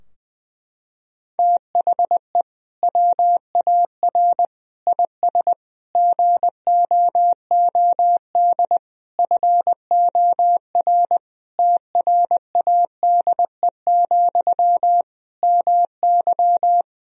Morse code, heard as a keyed tone, about 21 wpm, 700 Hz. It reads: THE WAR IS GOOD FOR TRADE, MY
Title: Sentences from Top 1000 Words - Review Part 4 - 20wpm